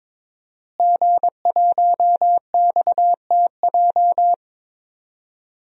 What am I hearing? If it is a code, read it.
Morse code, heard as a keyed tone, about 22 wpm, 700 Hz. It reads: G1XTJ
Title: DXCC-100 Call Signs - 5 Characters - Repeated 3x - 22wpm